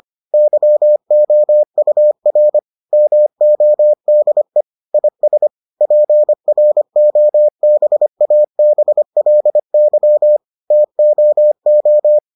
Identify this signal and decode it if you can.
Morse code, heard as a keyed tone, about 25 wpm, 600 Hz. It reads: YOUR MODE IS PROBABLY TOO